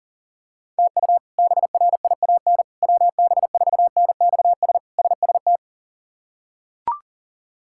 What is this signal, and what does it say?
Morse code, heard as a keyed tone, about 40 wpm, 700 Hz. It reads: TU BRIAN WB4NXS SST
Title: K1USN SST QSO Practice - 40wpm